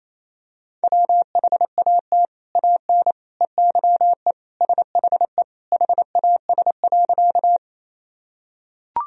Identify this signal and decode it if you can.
Morse code, heard as a keyed tone, about 28 wpm, 700 Hz. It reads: WHAT AN EYE SHE HAS.
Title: Sentences from Top 1000 Words - Review Part 1 - 28wpm